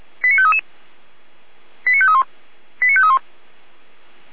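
Five-tone selective call (ZVEI), heard as quick runs of five tones, tones of about 80 ms. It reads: 89533 89531 89531
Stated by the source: ZVEI five-tone call received by SDR.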